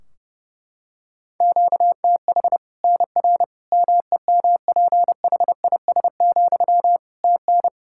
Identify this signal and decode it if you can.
Morse code, heard as a keyed tone, about 30 wpm, 700 Hz. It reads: QTH NR MEMPHIS, TN
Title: Practice QSO - Exchange 1 - Beginning - 30wpm